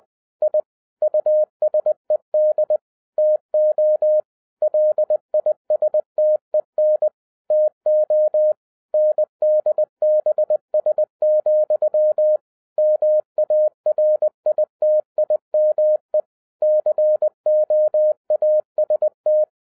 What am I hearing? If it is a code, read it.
Morse code, heard as a keyed tone, about 20 wpm, 610 Hz. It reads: I USED TO LISTEN TO NDBS, MARITIME COAST
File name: 2011-07-mp3 / n08 A LOT HAS CHANGED 20wpm